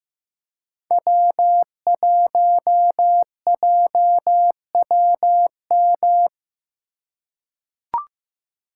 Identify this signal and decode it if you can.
Morse code, heard as a keyed tone, about 15 wpm, 700 Hz. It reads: W1JWM
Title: USA Call Signs - 2x Character Spacing - 15wpm